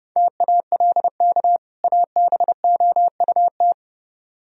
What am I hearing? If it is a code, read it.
Morse code, heard as a keyed tone, about 31 wpm, 700 Hz. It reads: TALK ABOUT